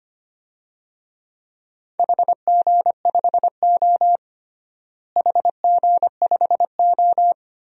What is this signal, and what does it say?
Morse code, heard as a keyed tone, about 25 wpm, 700 Hz. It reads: HG5O HG5O
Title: DXCC-100 Call Signs - 4 Characters - Repeated 2x - 25wpm